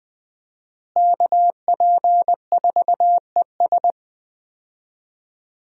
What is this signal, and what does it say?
Morse code, heard as a keyed tone, about 20 wpm, 700 Hz. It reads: KP4ES
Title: USA Call Signs - 5 Characters - Repeated 3x - 20wpm